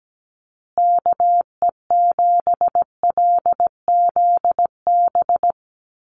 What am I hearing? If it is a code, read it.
Morse code, heard as a keyed tone, about 17 wpm, 700 Hz. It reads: KE7LZB
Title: USA Call Signs Repeated 2x 17wpm